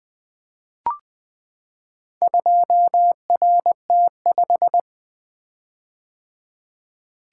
Morse code, heard as a keyed tone, about 20 wpm, 700 Hz. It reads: 2RT5
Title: Four Letter-Number Combinations - 20wpm